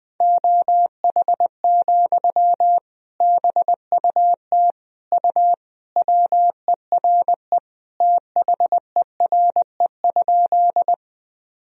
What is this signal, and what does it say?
Morse code, heard as a keyed tone, about 20 wpm, 700 Hz. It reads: OH, BUT U WERE THERE?
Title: Sentences from Top 500 Words and Abbreviations 20wpm